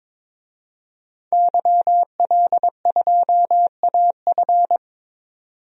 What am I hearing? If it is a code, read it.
Morse code, heard as a keyed tone, about 22 wpm, 700 Hz. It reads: YL2AF
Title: DXCC-100 Call Signs - 5 Characters - 22wpm